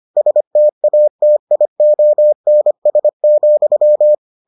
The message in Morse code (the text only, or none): STATIONS,